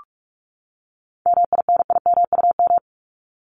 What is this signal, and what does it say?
Morse code, heard as a keyed tone, about 44 wpm, 700 Hz. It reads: MINIMUM
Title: Top 1251-1500 Words 45wpm